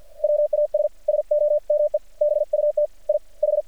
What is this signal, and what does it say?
Morse code, heard as a keyed tone, about 48 wpm, 600 Hz. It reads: WAS SOME BUT I H
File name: archived / CRSnet